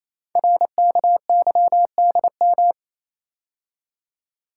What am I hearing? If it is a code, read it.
Morse code, heard as a keyed tone, about 27 wpm, 700 Hz. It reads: RKYDM
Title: Code Groups - 5 Letters - 28wpm